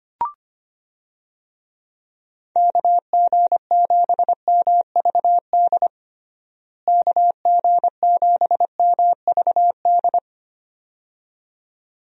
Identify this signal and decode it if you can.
Morse code, heard as a keyed tone, about 25 wpm, 700 Hz. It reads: KG7MVD KG7MVD
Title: USA Call Signs - 6 Characters - Repeated 2x - 25wpm